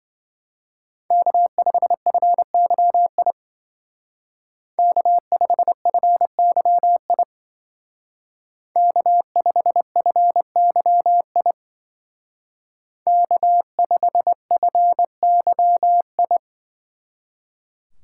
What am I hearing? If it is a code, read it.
Morse code, heard as a keyed tone, about 24 wpm, 700 Hz. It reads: K5FYI K5FYI K5FYI K5FYI
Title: Speed Racing - USA 1x3 Call Signs 20wpm